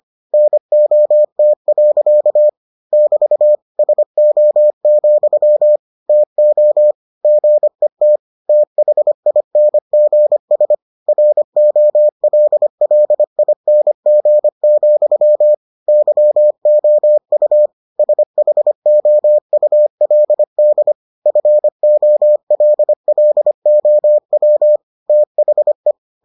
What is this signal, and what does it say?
Morse code, heard as a keyed tone, about 25 wpm, 600 Hz. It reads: NOT. <BT> SO, TO GET THINGS ROLLING, YOU SHOULD FOLLOW THE